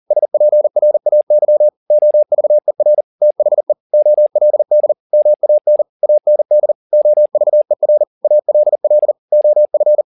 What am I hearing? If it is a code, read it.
Morse code, heard as a keyed tone, about 40 wpm, 600 Hz. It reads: SPRAY OVER THE OLD MAN AND OVER ALL OF
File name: the old man and the sea 2 40wpm